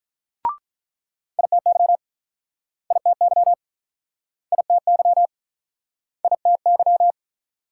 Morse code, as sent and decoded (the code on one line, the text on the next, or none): .. - -.-- / .. - -.-- / .. - -.-- / .. - -.--
ITY ITY ITY ITY